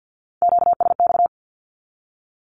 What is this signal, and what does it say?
Morse code, heard as a keyed tone, about 48 wpm, 700 Hz. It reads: QSX
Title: Q-Codes 50wpm